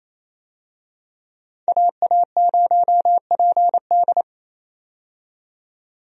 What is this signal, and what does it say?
Morse code, heard as a keyed tone, about 27 wpm, 700 Hz. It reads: AA0PD
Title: USA Call Signs - 5 Characters - 28wpm